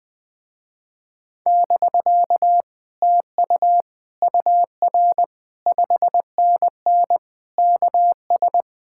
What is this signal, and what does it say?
Morse code, heard as a keyed tone, about 20 wpm, 700 Hz. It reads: <BK> TU UR 5NN KS
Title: Speed-Racing Common QSO Phrases - 20wpm